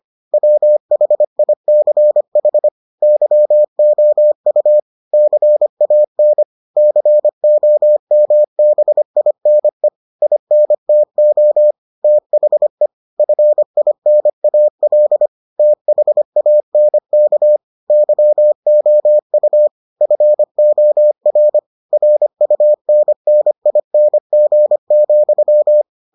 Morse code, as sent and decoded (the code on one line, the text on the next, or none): .-- .... .. -.-. .... / -.-- --- ..- / -.-. .- -. / -.-. --- -- -... .. -. / .. -. - --- / - .... / ..-. .. -. .- .-.. / - .... .- -. -.- / -.-- --- ..- / ..-. --- .-. / .-. ..- -. -. .. -. --. --..--
WHICH YOU CAN COMBINE INTO THE FINAL THANK YOU FOR RUNNING,